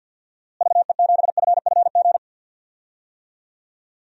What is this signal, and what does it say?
Morse code, heard as a keyed tone, about 46 wpm, 700 Hz. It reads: VE7FFG
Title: DXCC-100 Call Signs - 6 Characters - Repeated 2x - 50wpm